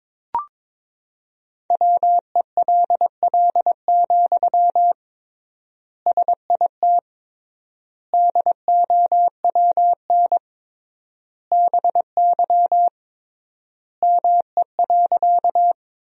Morse code, heard as a keyed tone, about 21 wpm, 700 Hz. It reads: WELL, SIT DOWN BY ME.
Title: 3x Word-Spacing - Sentences from Top 500 Words - 22wpm